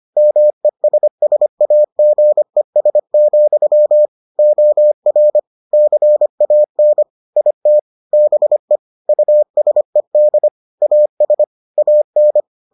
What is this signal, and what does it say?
Morse code, heard as a keyed tone, about 25 wpm, 600 Hz. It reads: MESSAGES, OR CAN IT BE USED AS AN